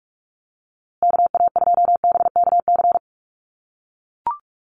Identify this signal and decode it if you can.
Morse code, heard as a keyed tone, about 44 wpm, 700 Hz. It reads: KA2BKC
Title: USA Call Signs Repeated 3x 45wpm